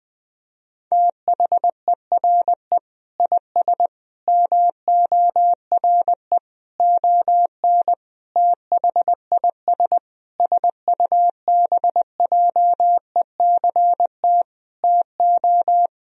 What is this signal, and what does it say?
Morse code, heard as a keyed tone, about 20 wpm, 700 Hz. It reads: THERE IS MORE ON THIS SUBJECT TO
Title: Sentences from Top 1000 Words - Review Part 1 - 20wpm